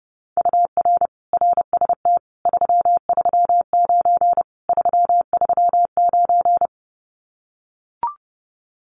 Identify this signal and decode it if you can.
Morse code, heard as a keyed tone, about 30 wpm, 700 Hz. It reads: UR RST 339 339
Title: Common QSO Phrases Encore 30wpm